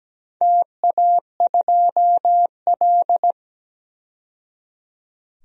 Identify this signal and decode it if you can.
Morse code, heard as a keyed tone, about 17 wpm, 700 Hz. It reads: TA2L
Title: DXCC-100 Call Signs - 4 Characters - 17wpm